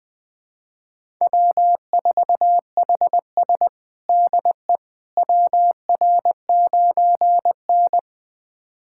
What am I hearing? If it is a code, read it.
Morse code, heard as a keyed tone, about 21 wpm, 700 Hz. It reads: W4HS DE WR9N
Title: USA Callsign DE Callsign - 4 Characters - 20wpm